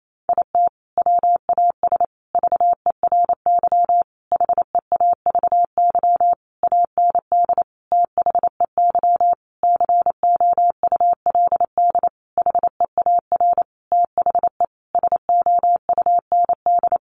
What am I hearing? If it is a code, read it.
Morse code, heard as a keyed tone, about 28 wpm, 700 Hz. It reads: IT WAS VERY HEAVY AND THEY COULD HEAR THE SOUND
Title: Sentences from Top 1100 Words 28wpm